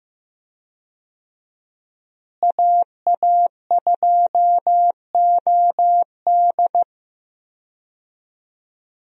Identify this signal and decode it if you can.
Morse code, heard as a keyed tone, about 15 wpm, 700 Hz. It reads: AA2OD
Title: USA Call Signs - 5 Characters - Repeated 3x - 15wpm